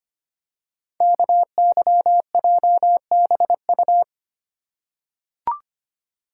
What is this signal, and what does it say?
Morse code, heard as a keyed tone, about 25 wpm, 700 Hz. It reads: KYJBU
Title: Code Groups - 5 Letters - 25wpm